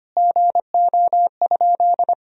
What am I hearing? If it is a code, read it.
Morse code, heard as a keyed tone, about 25 wpm, 700 Hz. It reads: GO?
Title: Sentences from Top 200 Words and Abbreviations 25wpm